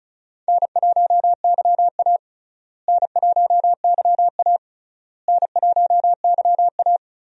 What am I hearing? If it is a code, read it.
Morse code, heard as a keyed tone, about 34 wpm, 700 Hz. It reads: N1YA N1YA N1YA
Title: USA Call Signs - 4 Characters - Repeated 3x - 35wpm